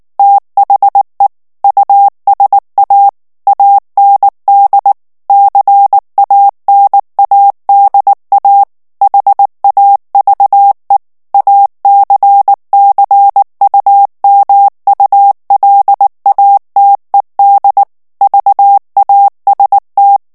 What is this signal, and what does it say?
Morse code, heard as a keyed tone, about 19 wpm, 800 Hz. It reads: THE USA AND CANADA HAVE ACCUMULATED VAST
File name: ANGLAIS texte clair 2 vitesse 18 mots minute